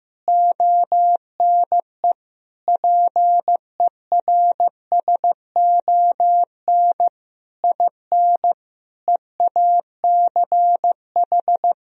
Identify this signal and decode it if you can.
Morse code, heard as a keyed tone, about 15 wpm, 700 Hz. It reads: ONE PERSON IN EACH
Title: Sentences from Top 1000 Words - Review Part 1 - 15wpm